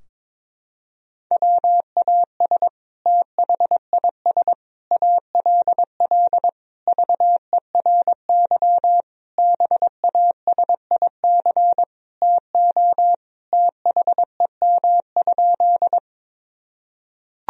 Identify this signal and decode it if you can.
Morse code, heard as a keyed tone, about 22 wpm, 700 Hz. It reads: WAS THIS ALL VERY BASIC TO THEM?
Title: Sentences from Top 1000 Words - Review Part 3 - 22wpm